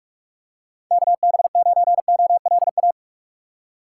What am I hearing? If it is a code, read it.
Morse code, heard as a keyed tone, about 46 wpm, 690 Hz. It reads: KB9OLA
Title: USA Call Signs - 6 Characters - 3x Character Spacing - 45wpm